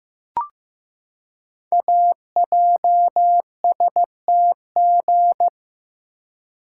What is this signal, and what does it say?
Morse code, heard as a keyed tone, about 15 wpm, 700 Hz. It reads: AJSTG
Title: Code Groups - 5 Letters - 15wpm